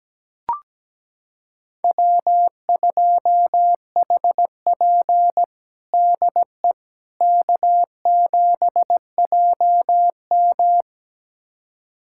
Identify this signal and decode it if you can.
Morse code, heard as a keyed tone, about 17 wpm, 700 Hz. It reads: W2HP DE K7JM
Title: USA Callsign DE Callsign - 4 Characters - 17wpm